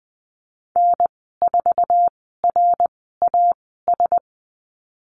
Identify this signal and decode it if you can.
Morse code, heard as a keyed tone, about 20 wpm, spaced as for 15 wpm, 700 Hz. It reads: N4RAS
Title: 167 Lesson 27g - Number 7 - Call Signs 20wpm